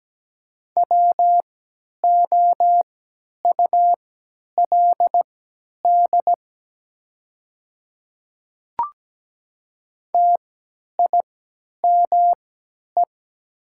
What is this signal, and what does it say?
Morse code, heard as a keyed tone, about 17 wpm, spaced as for 10 wpm, 700 Hz. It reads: WOULD TIME
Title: 3x Character Spacing - Top 100 Words - 17wpm